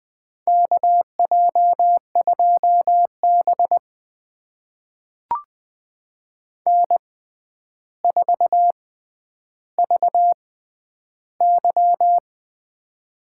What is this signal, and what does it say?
Morse code, heard as a keyed tone, about 20 wpm, spaced as for 10 wpm, 700 Hz. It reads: KJ2B N4VY
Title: USA Call Signs - 4 Characters - 6x Character Spacing - 20wpm